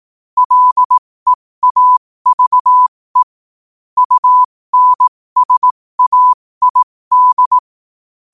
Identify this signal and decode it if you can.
Morse code, heard as a keyed tone, about 18 wpm, spaced as for 15 wpm, 1000 Hz. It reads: LEAVE UNSAID
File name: qotdfiles / 15wpm